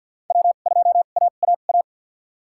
Morse code, heard as a keyed tone, about 47 wpm, spaced as for 35 wpm, 700 Hz. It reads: W2AAA
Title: USA Call Signs - 5 Characters - 2x Character Spacing - 50wpm